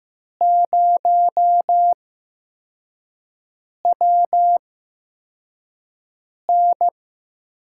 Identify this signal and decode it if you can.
Morse code, heard as a keyed tone, about 16 wpm, spaced as for 4 wpm, 700 Hz. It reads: 0WN
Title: USA Call Signs - 4 Characters - 8x Character Spacing - 15wpm